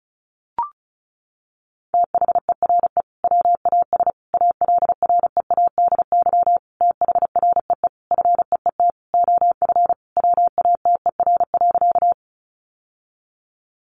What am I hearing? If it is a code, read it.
Morse code, heard as a keyed tone, about 34 wpm, 700 Hz. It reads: THERE WAS ALREADY THREE FEET OF WATER.
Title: Sentences from Top 700 Words 35wpm